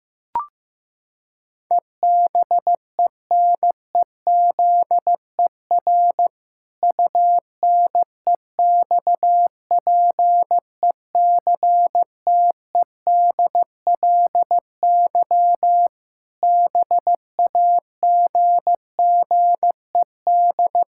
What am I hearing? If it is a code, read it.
Morse code, heard as a keyed tone, about 15 wpm, 700 Hz. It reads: EBENEZER UNEXPECTEDLY BAGGED
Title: Pangram Challenge v2 - 15wpm